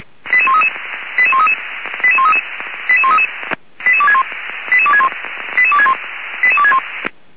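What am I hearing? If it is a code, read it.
Five-tone selective call (ZVEI), heard as quick runs of five tones, tones of about 70 ms. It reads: 88133 88133 88133 88133 88261 88261 88261 88261
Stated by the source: ZVEI five-tone selective call received via SDR.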